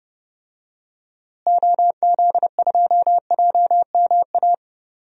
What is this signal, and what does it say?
Morse code, heard as a keyed tone, about 32 wpm, 700 Hz. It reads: OZ2JMA
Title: DXCC-100 Call Signs - 6 Characters - Repeated 3x - 30wpm